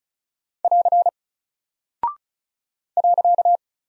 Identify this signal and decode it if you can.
Morse code, heard as a keyed tone, about 36 wpm, 700 Hz. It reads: <AR> .
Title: Single Character (Letter:Number:Punctuation:Prosign) Alt 35wpm